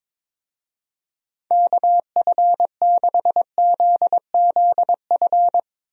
Morse code, heard as a keyed tone, about 22 wpm, 700 Hz. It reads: KF6ZZF
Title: USA Call Signs - 6 Characters - Repeated 3x - 22wpm